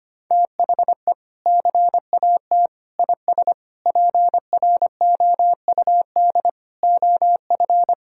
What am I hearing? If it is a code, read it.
Morse code, heard as a keyed tone, about 25 wpm, 700 Hz. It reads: THE CAT IS PROUD OF
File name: practice / 25wpm / Sentences with 2-Word Phrases - 25wpm